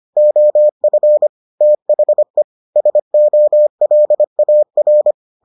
Morse code, heard as a keyed tone, about 25 wpm, 600 Hz. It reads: OF THE SOLAR